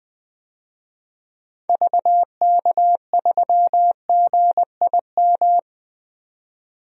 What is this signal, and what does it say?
Morse code, heard as a keyed tone, about 21 wpm, 700 Hz. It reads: VK3GIM
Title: DXCC-100 Call Signs - 6 Characters - 20wpm